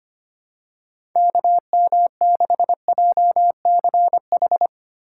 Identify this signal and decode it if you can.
Morse code, heard as a keyed tone, about 25 wpm, 700 Hz. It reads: KM6JCH
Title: USA Call Signs - 6 Characters - 4x Character Spacing - 25wpm